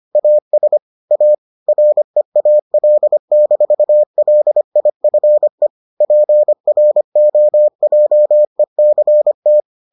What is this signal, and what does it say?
Morse code, heard as a keyed tone, about 25 wpm, 600 Hz. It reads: AS A REAL-LIFE PROJECT